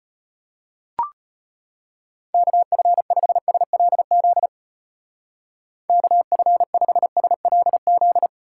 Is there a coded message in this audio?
Yes: Morse code from 2.3 to 8.3 s, heard as a keyed tone, about 31 wpm, 700 Hz. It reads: KF5SLZ KF5SLZ